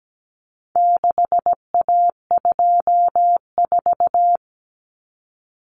Morse code, heard as a keyed tone, about 17 wpm, 700 Hz. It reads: 6A24
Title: Common Cut Numbers - 4-Digit 17wpm